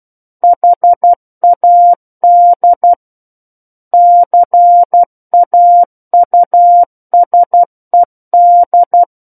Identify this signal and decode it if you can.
Morse code, heard as a keyed tone, about 12 wpm, 700 Hz. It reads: HAD CAUSED
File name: Book One / 03-Chapter 3 - 12 WPM